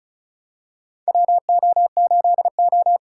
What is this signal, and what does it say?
Morse code, heard as a keyed tone, about 34 wpm, 700 Hz. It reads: WO8O
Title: USA Call Signs - 4 Characters - 4x Character Spacing - 35wpm